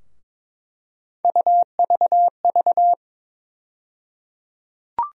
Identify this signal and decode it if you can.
Morse code, heard as a keyed tone, about 22 wpm, 700 Hz. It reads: UVV
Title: USA Call Sign Postfixes - 3 Letters 22wpm